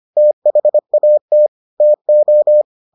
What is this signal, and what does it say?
Morse code, heard as a keyed tone, about 25 wpm, 600 Hz. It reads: THAT TO